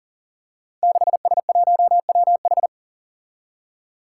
Morse code, heard as a keyed tone, about 38 wpm, 700 Hz. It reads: 6S1WH
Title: Five Challenging Letter-Number Combinations 40wpm